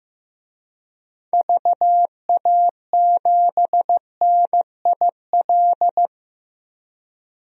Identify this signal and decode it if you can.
Morse code, heard as a keyed tone, about 15 wpm, 700 Hz. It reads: VA7NIL
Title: DXCC-100 Call Signs - 6 Characters - 15wpm